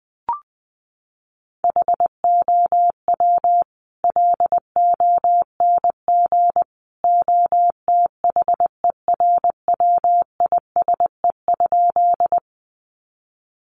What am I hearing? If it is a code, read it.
Morse code, heard as a keyed tone, about 20 wpm, 700 Hz. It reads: HOW LONG OTHERWISE?
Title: Sentences from Top 1200 Words 20wpm